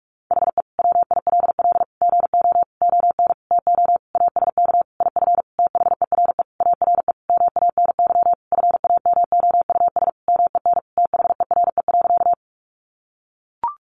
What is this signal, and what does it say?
Morse code, heard as a keyed tone, about 44 wpm, 700 Hz. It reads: HE WILL GO ON TO ASK IF THERE ARE MANY FAMOUS MEN THERE.
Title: Sentences from Top 1500 Words 45wpm